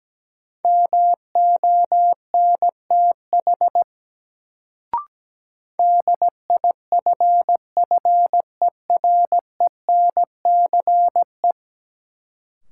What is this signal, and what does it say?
Morse code, heard as a keyed tone, about 17 wpm, 700 Hz. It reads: MONTH DIFFERENCE
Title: Sentences from Top 800 Words - New Word Drill - 17wpm